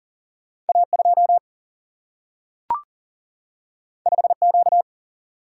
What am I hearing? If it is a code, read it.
Morse code, heard as a keyed tone, about 39 wpm, 700 Hz. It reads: A2 5Q
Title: Single-Letter Single-Number 40wpm